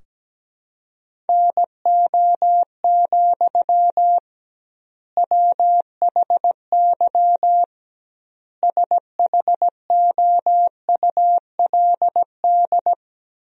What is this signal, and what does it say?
Morse code, heard as a keyed tone, about 17 wpm, 700 Hz. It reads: NO, WHY SHOULD
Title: 2x Word-Spacing - Sentences from Top 300 Words - 17wpm